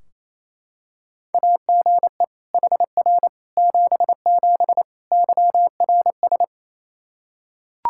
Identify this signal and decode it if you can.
Morse code, heard as a keyed tone, about 27 wpm, 700 Hz. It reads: AGE HR 77 YRS
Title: Practice QSO - Exchange 3 - More Info - 28wpm